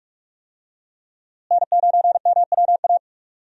Morse code, heard as a keyed tone, about 46 wpm, 700 Hz. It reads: N9MWA
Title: USA Call Signs - 5 Characters - Repeated 3x - 45wpm